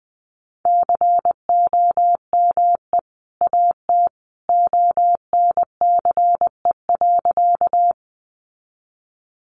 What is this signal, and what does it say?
Morse code, heard as a keyed tone, about 20 wpm, 700 Hz. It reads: COME AT ONCE.
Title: Sentences from Top 300 Words 20wpm